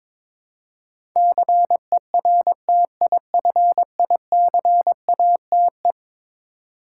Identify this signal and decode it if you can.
Morse code, heard as a keyed tone, about 22 wpm, 700 Hz. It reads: CERTIFICATE
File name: practice / 22wpm / Eleven Letter Words - 22wpm